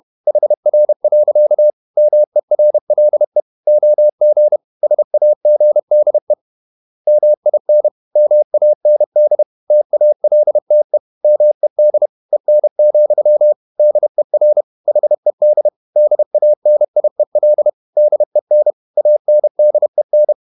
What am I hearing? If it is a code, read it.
Morse code, heard as a keyed tone, about 31 wpm, 600 Hz. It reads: HR. MERLE OG SAGDE MIN MAND TALTE MED EN, DER HED DANIEL DEN ANDEN
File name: Ferie QRQ 31wpm